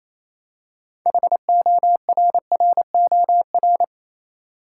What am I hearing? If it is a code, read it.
Morse code, heard as a keyed tone, about 28 wpm, 700 Hz. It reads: HORROR
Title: Six Letter Words - 28wpm